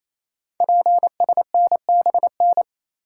Morse code, heard as a keyed tone, about 29 wpm, 690 Hz. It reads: PSNBN
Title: Code Groups - 5 Letters - 28wpm